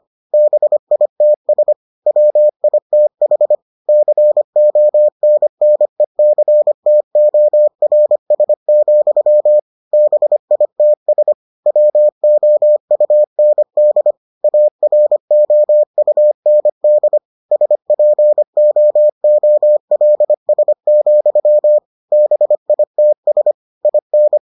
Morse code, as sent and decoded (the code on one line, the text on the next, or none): -... .. - ... / .-- .. - .... / -.-. --- -. -. -.-. - --- .-. ... --..-- / -... .. - ... / .-- --- ..- -. -.. / .- .-. --- ..- -. -.. / ... .--. --- --- .-.. ... --..-- / -... .. - ... / .. -.
BITS WITH CONNECTORS, BITS WOUND AROUND SPOOLS, BITS IN